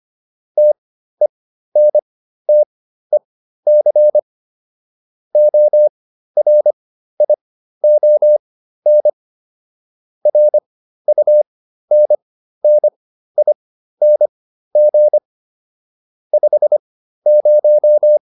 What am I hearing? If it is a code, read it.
Morse code, heard as a keyed tone, about 25 wpm, spaced as for 13 wpm, 600 Hz. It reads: TENTEC ORION RUNNING 50
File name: QSO402 13